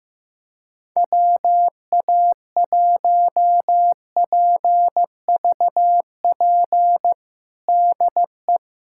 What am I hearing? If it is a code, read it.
Morse code, heard as a keyed tone, about 15 wpm, 700 Hz. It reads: WA1PVP DE
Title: USA Callsign DE Callsign - 6 Characters - 15wpm